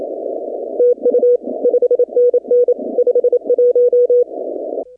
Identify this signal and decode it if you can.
Morse code, heard as a keyed tone, about 29 wpm, 500 Hz. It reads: TU 5NN 51
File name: contests / 2019-cw-wpx-kt5j-k5pi- -2330 0-15